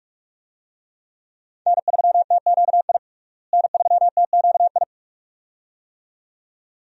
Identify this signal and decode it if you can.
Morse code, heard as a keyed tone, about 46 wpm, 700 Hz. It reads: N3TQI N3TQI
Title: USA Call Signs - 5 Characters - Repeated 2x - 45wpm